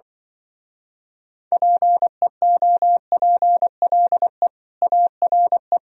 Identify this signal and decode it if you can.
Morse code, heard as a keyed tone, about 25 wpm, 700 Hz. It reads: PEOPLE ARE
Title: Speed-Racing - Sets of 4 Words 20wpm